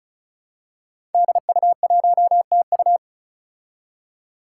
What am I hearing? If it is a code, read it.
Morse code, heard as a keyed tone, about 36 wpm, 700 Hz. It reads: DU1TU
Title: Five Challenging Letter-Number Combinations 35wpm